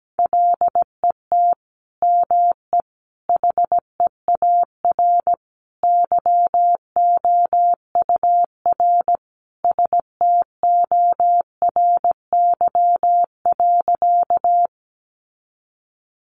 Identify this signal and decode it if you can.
Morse code, heard as a keyed tone, about 17 wpm, 700 Hz. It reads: LET ME HEAR YOUR STORY.
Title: Sentences from Top 300 Words 17wpm